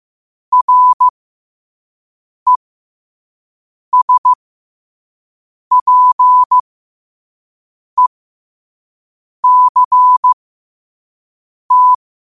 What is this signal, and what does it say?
Morse code, heard as a keyed tone, about 15 wpm, spaced as for 5 wpm, 1000 Hz. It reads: RESPECT
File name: qotdfiles / 5wpm